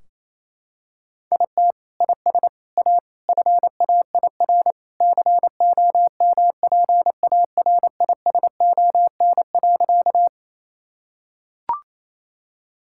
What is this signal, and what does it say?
Morse code, heard as a keyed tone, about 28 wpm, 700 Hz. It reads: IT IS A FAIR COMPARISON.
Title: Sentences from Top 1600 Words 28wpm